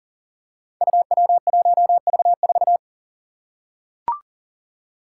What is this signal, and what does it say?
Morse code, heard as a keyed tone, about 41 wpm, 700 Hz. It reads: UW1V4
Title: Five Challenging Letter-Number Combinations 40wpm